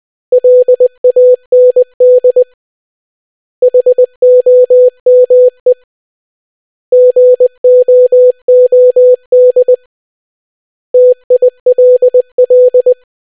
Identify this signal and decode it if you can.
Morse code, heard as a keyed tone, about 20 wpm, 500 Hz. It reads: LAND HOME GOOD TILL